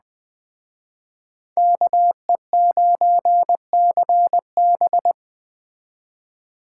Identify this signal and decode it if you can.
Morse code, heard as a keyed tone, about 20 wpm, 700 Hz. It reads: KE9CB